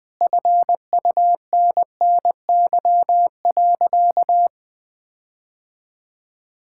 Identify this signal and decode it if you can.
Morse code, heard as a keyed tone, about 20 wpm, 700 Hz. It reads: FUNNY.
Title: All-Age Appropriate Jokes - 20wpm